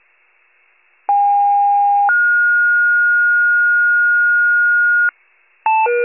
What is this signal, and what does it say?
Two-tone page, heard as a one-second tone, then a three-second tone, about 810 then 1430 Hz, with noise. It is heard from 1.1 to 5.1 s.